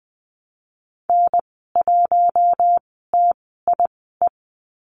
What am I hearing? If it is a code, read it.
Morse code, heard as a keyed tone, about 20 wpm, spaced as for 14 wpm, 700 Hz. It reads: N1TIE
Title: 042 Lesson 8g - Number 1 - Call Signs 20wpm